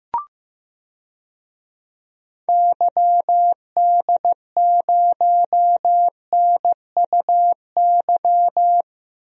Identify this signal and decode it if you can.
Morse code, heard as a keyed tone, about 15 wpm, 700 Hz. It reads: YD0NUY
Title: DXCC-100 Call Signs - 6 Characters - 15wpm